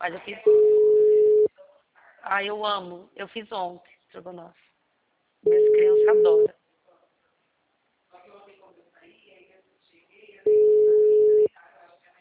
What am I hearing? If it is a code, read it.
Ringback tone (Europe, 425 Hz), with speech.
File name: out- -2007- - - .46417